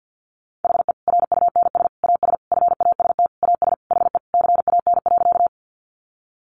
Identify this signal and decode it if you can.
Morse code, heard as a keyed tone, about 50 wpm, 700 Hz. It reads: HE RUNS AS FAST AS HE CAN.
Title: Additional Easy Phrases 50wpm